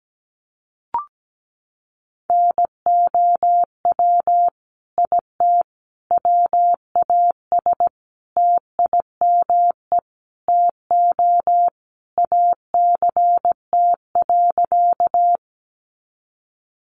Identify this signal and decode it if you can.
Morse code, heard as a keyed tone, about 17 wpm, 700 Hz. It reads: NOW IT WAS TIME TO ACT.
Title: Sentences from Top 200 Words 17wpm